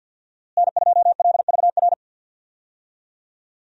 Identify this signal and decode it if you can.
Morse code, heard as a keyed tone, about 50 wpm, 700 Hz. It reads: N2LVR
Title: USA Call Signs - 5 Characters - Repeated 3x - 50wpm